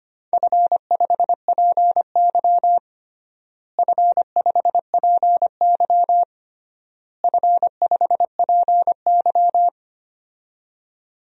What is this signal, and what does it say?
Morse code, heard as a keyed tone, about 25 wpm, 700 Hz. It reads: F5PY F5PY F5PY
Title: DXCC-100 Call Signs - 4 Characters - Repeated 3x - 25wpm